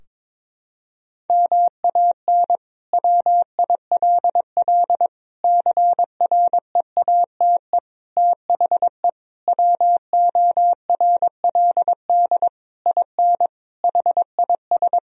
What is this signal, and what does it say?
Morse code, heard as a keyed tone, about 22 wpm, 700 Hz. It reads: MAN WILL CREATE THE WORLD IN HIS
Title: Sentences from Top 1000 Words - Review Part 4 - 22wpm